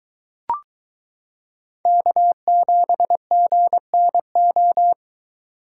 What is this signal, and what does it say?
Morse code, heard as a keyed tone, about 22 wpm, 700 Hz. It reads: K7GNO